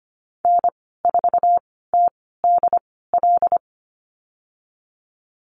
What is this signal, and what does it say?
Morse code, heard as a keyed tone, about 25 wpm, spaced as for 16 wpm, 700 Hz. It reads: N4TDL
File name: Lesson 15g - Number 5 - Call Signs 25wpm